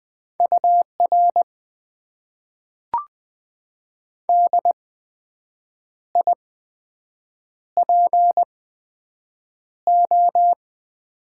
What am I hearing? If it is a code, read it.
Morse code, heard as a keyed tone, about 20 wpm, spaced as for 7 wpm, 700 Hz. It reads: UR DIPO
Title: Top 90 QSO Elements - 8x Character Spacing - 20wpm